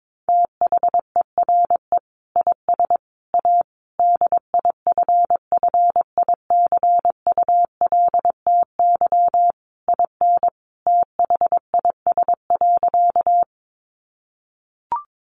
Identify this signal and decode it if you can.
Morse code, heard as a keyed tone, about 22 wpm, 700 Hz. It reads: THERE IS A DIFFICULTY IN THIS.
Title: Sentences from Top 1500 Words 22wpm